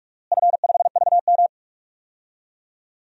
Morse code, heard as a keyed tone, about 44 wpm, 700 Hz. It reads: F5VM
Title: DXCC-100 Call Signs - 4 Characters - Repeated 2x - 45wpm